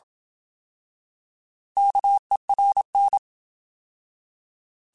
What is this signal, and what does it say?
Morse code, heard as a keyed tone, about 27 wpm, 800 Hz. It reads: KERN
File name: Alphabetical / Alphabetical mp3s / Ks @25